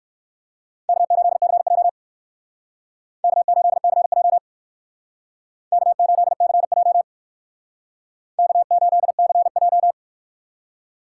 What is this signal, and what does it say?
Morse code, heard as a keyed tone, about 57 wpm, 700 Hz. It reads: K8CJ K8CJ K8CJ K8CJ